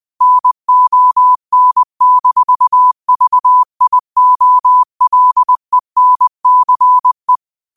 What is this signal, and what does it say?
Morse code, heard as a keyed tone, about 20 wpm, 1000 Hz. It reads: NON-VIOLENCE
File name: qotdfiles / 20wpm